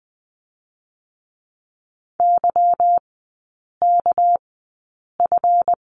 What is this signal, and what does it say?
Morse code, heard as a keyed tone, about 19 wpm, spaced as for 8 wpm, 690 Hz. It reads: YKF
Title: 205 Lesson 33e - Letter j - Three Characters 20wpm